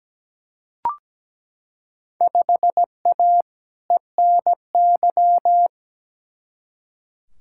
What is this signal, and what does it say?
Morse code, heard as a keyed tone, about 17 wpm, 700 Hz. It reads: 5A ENY
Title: Field Day Exchanges Encore - 17wpm